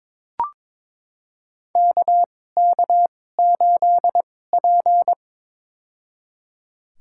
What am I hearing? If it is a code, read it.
Morse code, heard as a keyed tone, about 22 wpm, spaced as for 16 wpm, 700 Hz. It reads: KK8P
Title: USA Call Signs - 4 Characters - 2x Character Spacing - 22wpm